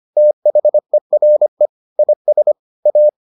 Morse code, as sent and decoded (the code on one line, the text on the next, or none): - .... .-. / .. ... / .-
THERE IS A